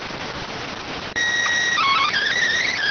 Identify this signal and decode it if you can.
SSTV (slow-scan television), PD 290.